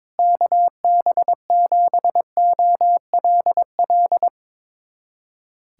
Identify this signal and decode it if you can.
Morse code, heard as a keyed tone, about 22 wpm, 700 Hz. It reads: KB7OLL
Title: USA Callsign DE Callsign - 6 Characters - 22wpm